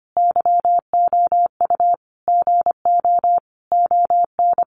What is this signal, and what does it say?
Morse code, heard as a keyed tone, about 25 wpm, 700 Hz. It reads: YOU GO ON
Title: Sentences from Top 1100 Words 25wpm